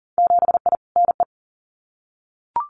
Morse code, heard as a keyed tone, about 40 wpm, 700 Hz. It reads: 7I NE
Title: Winter Field Day Exchanges Encore 40wpm